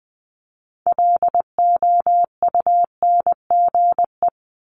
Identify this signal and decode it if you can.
Morse code, heard as a keyed tone, about 20 wpm, 700 Hz. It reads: LOUNGE
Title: Multi-letter Phonemes 20wpm